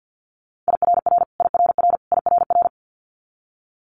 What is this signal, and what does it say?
Morse code, heard as a keyed tone, about 50 wpm, 700 Hz. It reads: IRR IRR IRR
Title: Multi-letter Phonemes Repeated 3x 50wpm